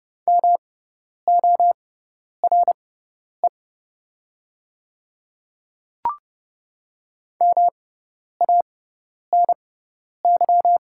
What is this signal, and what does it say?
Morse code, heard as a keyed tone, about 30 wpm, spaced as for 10 wpm, 700 Hz. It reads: MORE MANY
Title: 6x Character Spacing - Top 100 Words - 30wpm